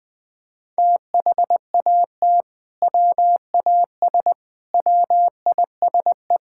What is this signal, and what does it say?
Morse code, heard as a keyed tone, about 20 wpm, 700 Hz. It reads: THAT WAS WISE
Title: Sentences from Top 1600 Words 20wpm